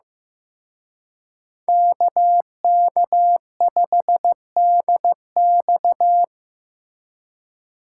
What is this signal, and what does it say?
Morse code, heard as a keyed tone, about 18 wpm, 700 Hz. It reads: KK5DX
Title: Speed Racing - USA 2x2 Call Signs 15wpm